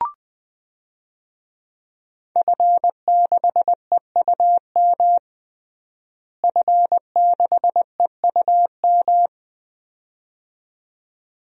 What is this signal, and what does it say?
Morse code, heard as a keyed tone, about 20 wpm, 700 Hz. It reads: F6EUM F6EUM
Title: DXCC-100 Call Signs - 5 Characters - Repeated 2x - 20wpm